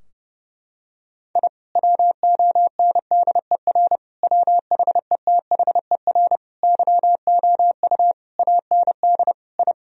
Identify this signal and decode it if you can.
Morse code, heard as a keyed tone, about 30 wpm, 700 Hz. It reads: I WONDER WHETHER YOU AND I
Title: Sentences from Top 1000 Words - Review Part 3 - 30wpm